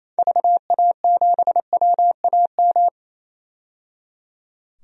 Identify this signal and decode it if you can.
Morse code, heard as a keyed tone, about 28 wpm, 700 Hz. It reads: VA7WAM
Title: DXCC-100 Call Signs - 6 Characters - 28wpm